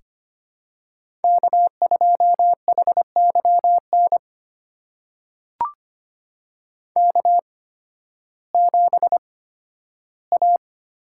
Morse code, heard as a keyed tone, about 25 wpm, spaced as for 13 wpm, 700 Hz. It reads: K2HYN K7A
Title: USA Call Signs - 5 Characters - 8x Character Spacing - 25wpm